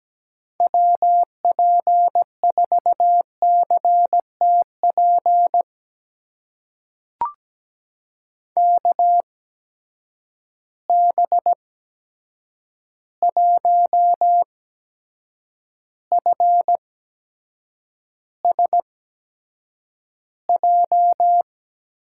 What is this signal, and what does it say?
Morse code, heard as a keyed tone, about 17 wpm, spaced as for 7 wpm, 700 Hz. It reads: WP4CTP KB1FSJ
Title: USA Call Signs - 6 Characters - 8x Character Spacing - 17wpm